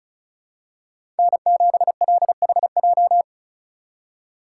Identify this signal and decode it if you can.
Morse code, heard as a keyed tone, about 36 wpm, 700 Hz. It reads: N7LHJ
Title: USA Call Signs - 4x Character Spacing - 35wpm